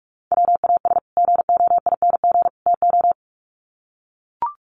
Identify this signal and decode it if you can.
Morse code, heard as a keyed tone, about 44 wpm, 700 Hz. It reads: WAS GOING TO
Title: Sets of 3 Words 45wpm